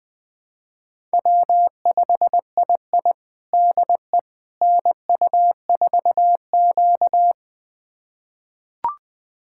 Morse code, heard as a keyed tone, about 20 wpm, 700 Hz. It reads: W5II DE NU4Q
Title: USA Callsign DE Callsign - 4 Characters - 20wpm